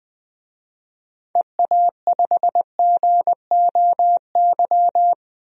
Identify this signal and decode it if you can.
Morse code, heard as a keyed tone, about 20 wpm, 700 Hz. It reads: EA5GOY
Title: DXCC-100 Call Signs - 6 Characters - Repeated 3x - 20wpm